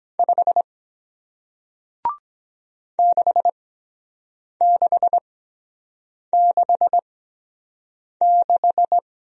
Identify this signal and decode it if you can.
Morse code, heard as a keyed tone, about 24 wpm, spaced as for 7 wpm, 700 Hz. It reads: 5 6666